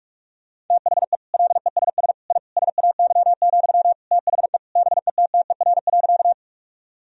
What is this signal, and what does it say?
Morse code, heard as a keyed tone, about 46 wpm, 700 Hz. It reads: THE LESS I SAY, THE BETTER.
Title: Sentences from Top 1000 Words - Review Part 2 - 45wpm